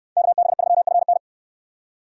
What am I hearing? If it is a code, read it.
Morse code, heard as a keyed tone, about 64 wpm, 700 Hz. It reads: KB3FN